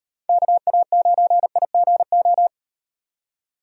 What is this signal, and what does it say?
Morse code, heard as a keyed tone, about 38 wpm, 700 Hz. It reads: KA9IGO